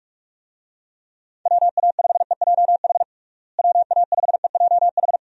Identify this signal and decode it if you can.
Morse code, heard as a keyed tone, about 46 wpm, 700 Hz. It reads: WA5EJH WA5EJH
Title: USA Call Signs - 6 Characters - Repeated 2x - 45wpm